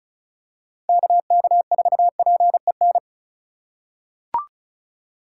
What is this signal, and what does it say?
Morse code, heard as a keyed tone, about 34 wpm, 700 Hz. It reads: KK4PEN
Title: USA Call Signs - 6 Characters - 2x Character Spacing - 35wpm